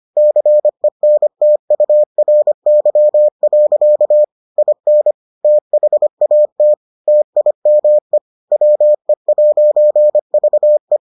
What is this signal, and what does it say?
Morse code, heard as a keyed tone, about 25 wpm, 600 Hz. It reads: CENTURY. IN THAT TIME WE'VE